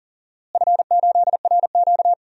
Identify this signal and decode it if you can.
Morse code, heard as a keyed tone, about 37 wpm, 700 Hz. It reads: F8RQ
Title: DXCC-100 Call Signs - 4 Characters - Repeated 3x - 40wpm